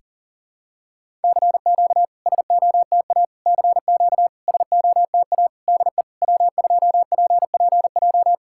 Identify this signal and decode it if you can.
Morse code, heard as a keyed tone, about 44 wpm, 700 Hz. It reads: CQ SOTA CQ SOTA DE W2PPJ
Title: SOTA QSO Practice 40wpm